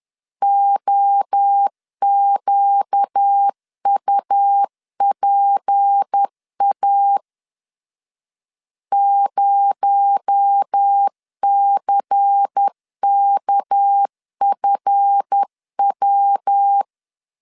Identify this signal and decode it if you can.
Morse code, heard as a keyed tone, about 10 wpm, 800 Hz. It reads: OQUPA 0CKFW
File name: LEZIONI / 8 YJ45 / 8 Y 10wpm 2615